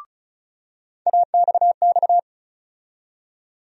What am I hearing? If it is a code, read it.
Morse code, heard as a keyed tone, about 36 wpm, 700 Hz. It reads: AXX
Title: USA Call Sign Postfixes - 3 Letters 35wpm